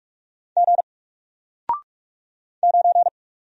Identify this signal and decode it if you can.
Morse code, heard as a keyed tone, about 44 wpm, 700 Hz. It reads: G 9
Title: Single Character (Letter:Number:Punctuation:Prosign) Alt 45wpm